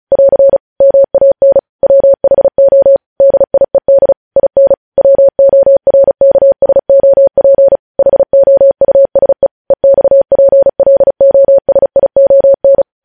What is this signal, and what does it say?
Morse code, heard as a keyed tone, about 36 wpm, 550 Hz. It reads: <AR> MAN WHO DIED IN WORKSOP HOUSE EXPLOSION
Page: news headlines in morse code at 35 wpm